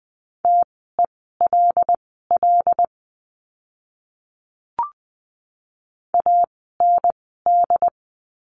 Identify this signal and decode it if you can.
Morse code, heard as a keyed tone, about 20 wpm, spaced as for 14 wpm, 700 Hz. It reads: TELL AND
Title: 115 Lesson 19f - Letter w - Words 20wpm